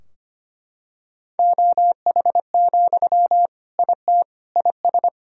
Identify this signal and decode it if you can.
Morse code, heard as a keyed tone, about 25 wpm, 700 Hz. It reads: OH, IT IS
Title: Sentences from Top 1000 Words - Review Part 3 - 25wpm